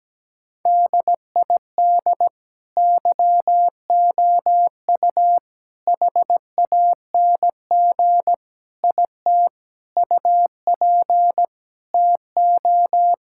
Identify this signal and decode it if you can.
Morse code, heard as a keyed tone, about 17 wpm, 700 Hz. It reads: DID YOU HANG IT UP TO
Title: Sentences from Top 1600 Words 17wpm